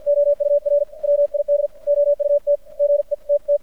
Morse code, heard as a keyed tone, about 55 wpm, 580 Hz. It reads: OUR WID OUT G E T T
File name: archived / CRSnet